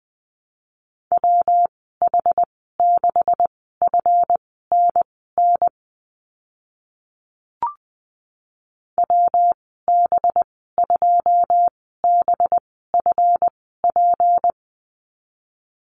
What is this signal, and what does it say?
Morse code, heard as a keyed tone, about 20 wpm, spaced as for 14 wpm, 700 Hz. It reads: WH6FNN WB2BFP
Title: 238 Lesson 38g - Letter q - Call Signs 20wpm